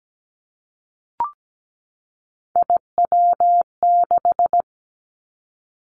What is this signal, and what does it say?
Morse code, heard as a keyed tone, about 17 wpm, 700 Hz. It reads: IW6
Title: Three Challenging Letter-Number Combinations 17wpm